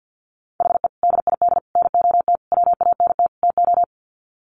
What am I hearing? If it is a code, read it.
Morse code, heard as a keyed tone, about 50 wpm, 700 Hz. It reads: HE DID NOT WANT TO
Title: Sets of 5 Words 50wpm